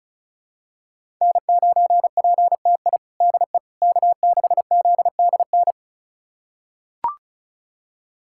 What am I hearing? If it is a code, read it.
Morse code, heard as a keyed tone, about 34 wpm, 700 Hz. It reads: N9PTI DE K6ZDN
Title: USA Callsign DE Callsign - 5 Characters - 35wpm